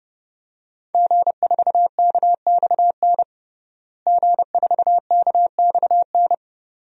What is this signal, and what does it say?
Morse code, heard as a keyed tone, about 30 wpm, 700 Hz. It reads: G4KXN G4KXN
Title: DXCC-100 Call Signs - 5 Characters - Repeated 2x - 30wpm